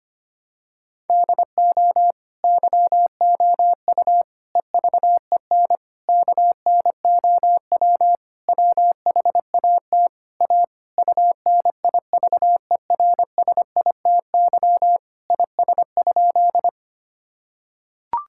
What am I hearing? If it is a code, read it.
Morse code, heard as a keyed tone, about 25 wpm, 700 Hz. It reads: DO YOU EVEN KNOW WHAT A UNIVERSITY IS?
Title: Sentences from Top 1000 Words - Review Part 5 - 25wpm